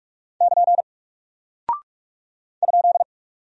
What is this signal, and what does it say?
Morse code, heard as a keyed tone, about 44 wpm, 700 Hz. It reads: <KN> ?
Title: Single Character (Letter:Number:Punctuation:Prosign) Alt 45wpm